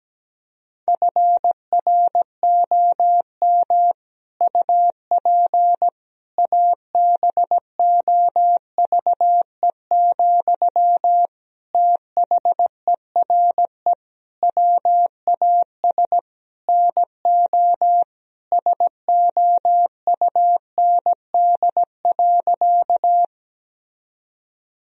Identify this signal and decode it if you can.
Morse code, heard as a keyed tone, about 17 wpm, 700 Hz. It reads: FROM UP ABOVE, THERE WAS NO SOUND.
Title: Sentences from Top 1000 Words - Review Part 2 - 17wpm